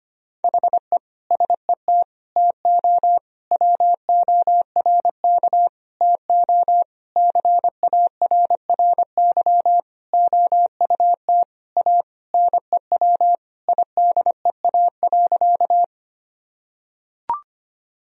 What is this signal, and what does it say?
Morse code, heard as a keyed tone, about 25 wpm, 700 Hz. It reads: HE SET TO WORK TO CARRY OUT A NEW IDEA.
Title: Sentences from Top 1000 Words - Review Part 1 - 25wpm